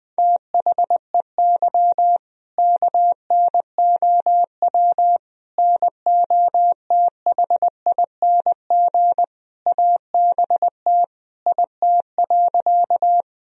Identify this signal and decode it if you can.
Morse code, heard as a keyed tone, about 21 wpm, 700 Hz. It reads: THEY KNOW NOTHING ABT IT.
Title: Sentences from Top 500 Words and Abbreviations 20wpm